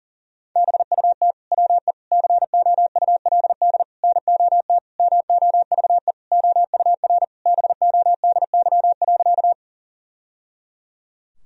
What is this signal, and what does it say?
Morse code, heard as a keyed tone, about 40 wpm, 700 Hz. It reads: BUT WE COULD NOT MOVE OUR BODY.
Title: Sentences from Top 1000 Words - Review Part 2 - 40wpm